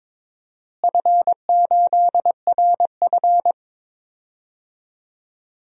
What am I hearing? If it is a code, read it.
Morse code, heard as a keyed tone, about 22 wpm, 700 Hz. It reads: F8RF
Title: DXCC-100 Call Signs - 4 Characters - Repeated 3x - 22wpm